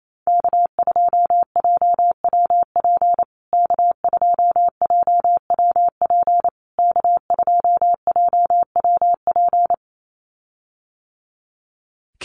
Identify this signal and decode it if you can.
Morse code, heard as a keyed tone, about 28 wpm, 700 Hz. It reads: K2JWP K2JWP K2JWP
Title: USA Call Signs Repeated 3x 28wpm